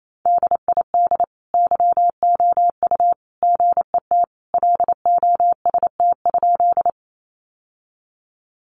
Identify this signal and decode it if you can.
Morse code, heard as a keyed tone, about 28 wpm, 700 Hz. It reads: DID YOU GET LOST?
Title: Sentences from Top 1400 Words 28wpm